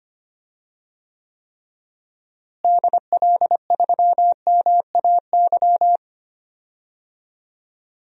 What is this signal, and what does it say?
Morse code, heard as a keyed tone, about 25 wpm, 700 Hz. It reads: DL3MAY
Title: DXCC-100 Call Signs - 6 Characters - Repeated 3x - 25wpm